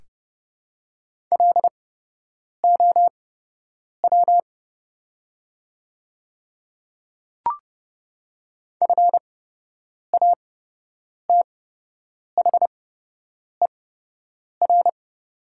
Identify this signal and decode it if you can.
Morse code, heard as a keyed tone, about 30 wpm, spaced as for 8 wpm, 700 Hz. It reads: LOW FATHER
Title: 8x Character Spacing - Top 300 Words - 30wpm